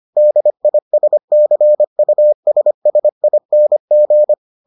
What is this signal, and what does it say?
Morse code, heard as a keyed tone, about 25 wpm, 600 Hz. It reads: DISCUSSING